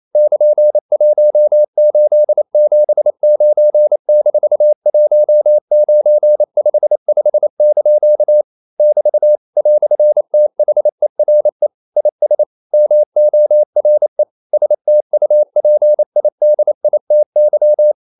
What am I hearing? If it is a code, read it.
Morse code, heard as a keyed tone, about 28 wpm, 600 Hz. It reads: <KN>1879-1955) <BT> "THERE IS MORE STUPIDITY
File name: CW / Practice / EnglishQuotations0228